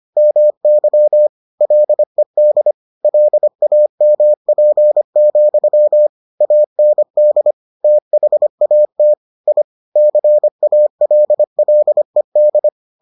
Morse code, heard as a keyed tone, about 25 wpm, 600 Hz. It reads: MY LED LAMP, AND THAT I CALLED